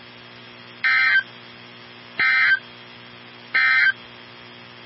Emergency Alert System (SAME) tones, heard as harsh buzzing bursts. It is heard from 0.7 to 4.0 s.
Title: NOAA Weather Radio 162.550